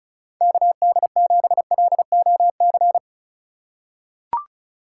Morse code, heard as a keyed tone, about 36 wpm, 700 Hz. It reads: KD7LOC
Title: USA Call Signs - 6 Characters - 6x Character Spacing - 35wpm